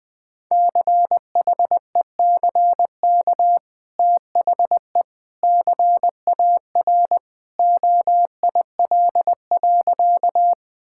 Morse code, heard as a keyed tone, about 20 wpm, 700 Hz. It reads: CHECK THE CAR OIL.
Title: Easy Sentences with Hints - 20wpm